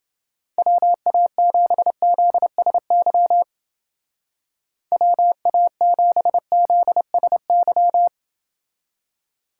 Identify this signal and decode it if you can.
Morse code, heard as a keyed tone, about 28 wpm, 700 Hz. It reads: WA7ZSY WA7ZSY